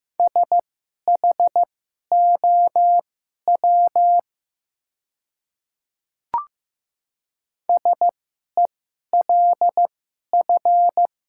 Morse code, heard as a keyed tone, about 15 wpm, spaced as for 11 wpm, 700 Hz. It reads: SHOW SELF
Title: 2x Character Spacing - Top 500 Words - 15wpm